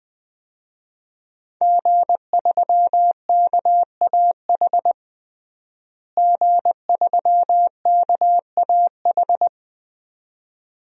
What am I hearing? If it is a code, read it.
Morse code, heard as a keyed tone, about 21 wpm, 700 Hz. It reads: G3KAH G3KAH
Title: DXCC-100 Call Signs - 5 Characters - Repeated 2x - 20wpm